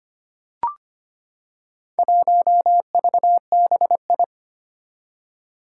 Morse code, heard as a keyed tone, about 25 wpm, 700 Hz. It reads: 1VBI
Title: Four Challenging Letter-Number Combinations 25wpm